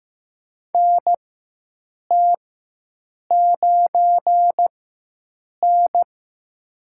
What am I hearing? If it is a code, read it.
Morse code, heard as a keyed tone, about 15 wpm, spaced as for 7 wpm, 700 Hz. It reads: NT9N
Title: USA Call Signs - 4 Characters - 4x Character Spacing - 15wpm